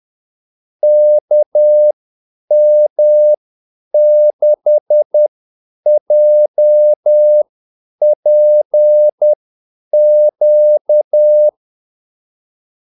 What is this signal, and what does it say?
Morse code, heard as a keyed tone, about 10 wpm, spaced as for 8 wpm, 600 Hz. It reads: KM6JPQ
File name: Callsigns-10-8wpm